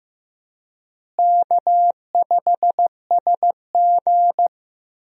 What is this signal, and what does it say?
Morse code, heard as a keyed tone, about 15 wpm, 700 Hz. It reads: K5SG
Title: USA Call Signs - 4 Characters - Repeated 3x - 15wpm